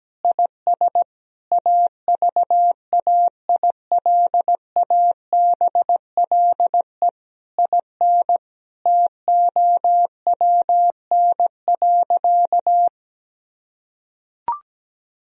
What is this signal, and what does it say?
Morse code, heard as a keyed tone, about 17 wpm, 700 Hz. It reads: IS AVAILABLE IN TOWN.
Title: Sentences from Top 1000 Words - Review Part 3 - 17wpm